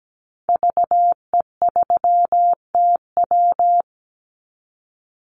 Morse code, heard as a keyed tone, about 17 wpm, 700 Hz. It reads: VE3TW
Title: Call Signs Repeated 3x - 17wpm